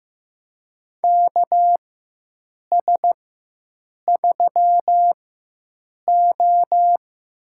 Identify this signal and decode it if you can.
Morse code, heard as a keyed tone, about 15 wpm, spaced as for 7 wpm, 700 Hz. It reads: KS3O
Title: USA Call Signs - 4x Character Spacing - 15wpm